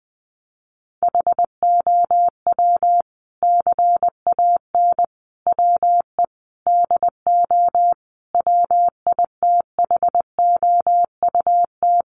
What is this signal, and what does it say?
Morse code, heard as a keyed tone, about 20 wpm, 700 Hz. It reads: HOW CAN WE DO WITHOUT
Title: Sentences from Top 600 Words 20wpm